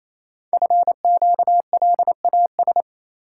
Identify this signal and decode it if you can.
Morse code, heard as a keyed tone, about 29 wpm, 700 Hz. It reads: FQLAS
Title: Code Groups - 5 Letters - 28wpm